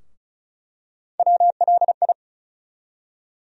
Morse code, heard as a keyed tone, about 34 wpm, 700 Hz. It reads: WLI
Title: USA Call Sign Postfixes - 3 Letters 35wpm